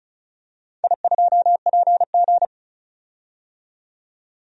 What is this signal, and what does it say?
Morse code, heard as a keyed tone, about 34 wpm, 700 Hz. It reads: I2PG
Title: DXCC-100 Call Signs - 4 Characters - 35wpm